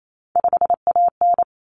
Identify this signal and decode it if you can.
Morse code, heard as a keyed tone, about 28 wpm, 700 Hz. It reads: 5AN
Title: Common Cut Numbers - 3-Digit 28wpm